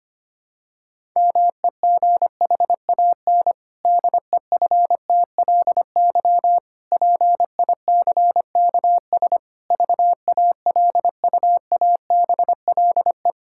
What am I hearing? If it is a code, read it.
Morse code, heard as a keyed tone, about 25 wpm, 700 Hz. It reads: MEGHAN DEFTLY PICKS VALUABLE
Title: Pangram Challenge v2 - 25wpm